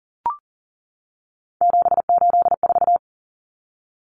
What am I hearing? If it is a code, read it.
Morse code, heard as a keyed tone, about 40 wpm, 700 Hz. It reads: 784
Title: Numbers - 3-Digit 40wpm